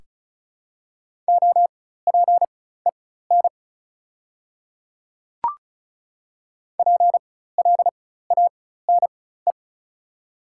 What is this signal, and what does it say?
Morse code, heard as a keyed tone, about 34 wpm, spaced as for 16 wpm, 690 Hz. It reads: OPEN PLANE
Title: 4x Character Spacing - Top 500 Words - 35wpm